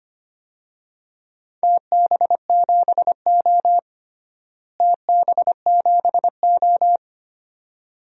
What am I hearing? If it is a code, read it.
Morse code, heard as a keyed tone, about 25 wpm, 700 Hz. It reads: TB7O TB7O
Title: DXCC-100 Call Signs - 4 Characters - Repeated 2x - 25wpm